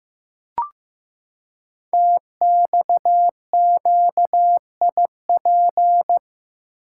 Morse code, heard as a keyed tone, about 15 wpm, 700 Hz. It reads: TXQIP
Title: Code Groups - 5 Letters - 15wpm